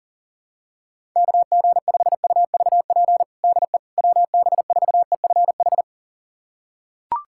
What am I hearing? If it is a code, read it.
Morse code, heard as a keyed tone, about 40 wpm, 700 Hz. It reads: KG5UVP DE WB4EFH
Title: USA Callsign DE Callsign 40wpm